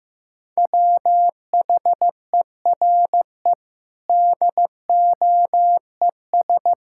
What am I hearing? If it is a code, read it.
Morse code, heard as a keyed tone, about 16 wpm, 700 Hz. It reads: WHERE DOES
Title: Sentences from Top 1000 Words - Review Part 6 - 15wpm